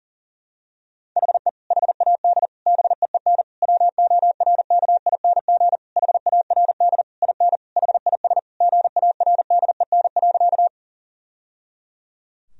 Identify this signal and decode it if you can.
Morse code, heard as a keyed tone, about 37 wpm, 700 Hz. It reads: HE HAD BEEN WORKING HARD IN HIS GARDEN.
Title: Sentences from Top 1000 Words - Review Part 4 - 40wpm